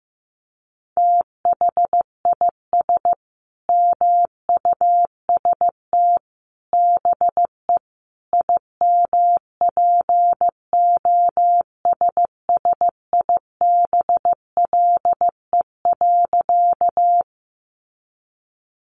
Morse code, heard as a keyed tone, about 15 wpm, 700 Hz. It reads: THIS MUST BE IMPOSSIBLE.
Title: Sentences from Top 1100 Words 15wpm